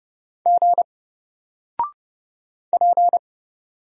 Morse code, heard as a keyed tone, about 31 wpm, 700 Hz. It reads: G P
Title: Single Character (Letter:Number:Punctuation:Prosign) 30wpm